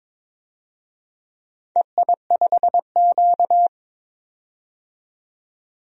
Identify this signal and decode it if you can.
Morse code, heard as a keyed tone, about 22 wpm, 700 Hz. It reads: EI5Q
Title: DXCC-100 Call Signs - 4 Characters - 22wpm